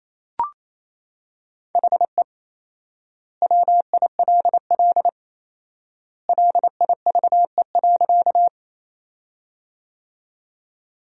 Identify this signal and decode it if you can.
Morse code, heard as a keyed tone, about 28 wpm, 700 Hz. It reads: HE WILL LIVE.
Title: 4x Word-Spacing - Sentences from Top 200 Words - 28wpm